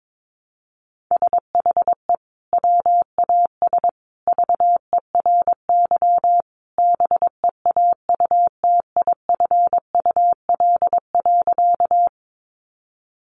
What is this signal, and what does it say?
Morse code, heard as a keyed tone, about 22 wpm, 700 Hz. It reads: SHE WAS VERY BEAUTIFUL.
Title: Sentences from Top 900 Words 22wpm